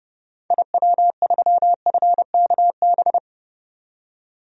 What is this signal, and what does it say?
Morse code, heard as a keyed tone, about 32 wpm, 700 Hz. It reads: IW3FKB
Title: DXCC-100 Call Signs - 6 Characters - 30wpm